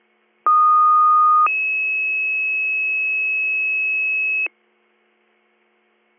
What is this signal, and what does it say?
Two-tone page, heard as a one-second tone, then a three-second tone, about 1230 then 2600 Hz. It is heard from 0.5 to 4.5 s.